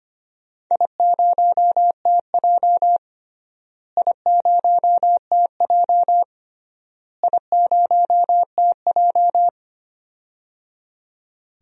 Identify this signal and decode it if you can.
Morse code, heard as a keyed tone, about 25 wpm, 700 Hz. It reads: I0TJ I0TJ I0TJ
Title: DXCC-100 Call Signs - 4 Characters - Repeated 3x - 25wpm